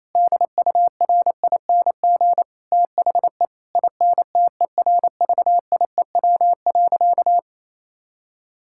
Morse code, heard as a keyed tone, about 28 wpm, 700 Hz. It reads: DURING THE INTERVIEW.
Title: Sentences from Top 1600 Words 28wpm